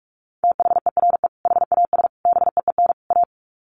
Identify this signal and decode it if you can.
Morse code, heard as a keyed tone, about 44 wpm, 700 Hz. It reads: THERE HAS BEEN A
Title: Sets of 4 Words 45wpm